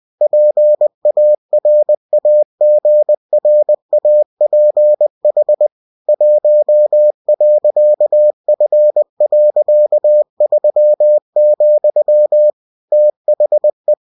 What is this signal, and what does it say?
Morse code, heard as a keyed tone, about 20 wpm, 600 Hz. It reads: PARAGRAPH 1.F.3, THE